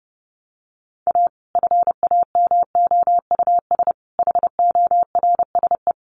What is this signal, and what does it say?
Morse code, heard as a keyed tone, about 30 wpm, 700 Hz. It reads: A FAMOUS HORSE
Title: Sentences from Top 1500 Words 30wpm